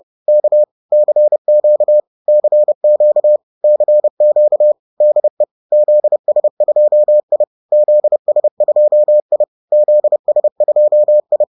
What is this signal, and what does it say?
Morse code, heard as a keyed tone, about 30 wpm, 600 Hz. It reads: K CQ CQ CQ DE ZS2I ZS2I ZS2I